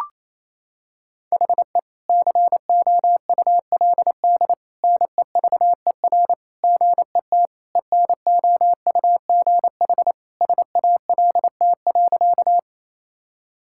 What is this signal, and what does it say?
Morse code, heard as a keyed tone, about 28 wpm, 700 Hz. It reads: HE COULD NEVER GET ENOUGH SALT.
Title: Sentences from Top 1000 Words - Review Part 5 - 28wpm